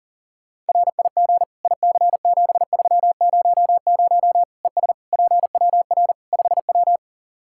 Morse code, heard as a keyed tone, about 40 wpm, 700 Hz. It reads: RIG IC7300 ES PWR 5W
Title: Practice QSO - Exchange 2 - Info - 40wpm